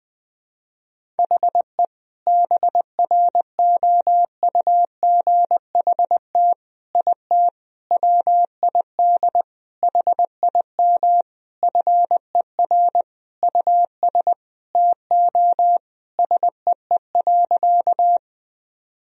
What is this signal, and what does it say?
Morse code, heard as a keyed tone, about 20 wpm, 700 Hz. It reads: HE BROUGHT IT WID HIM FER US TO SEE.
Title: Sentences from Top 500 Words and Abbreviations 20wpm